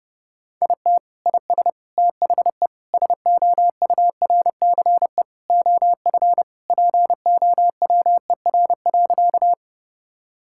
Morse code, heard as a keyed tone, about 30 wpm, 700 Hz. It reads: IT IS THE SOURCE OF POWER.
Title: Sentences from Top 1000 Words - Review Part 5 - 30wpm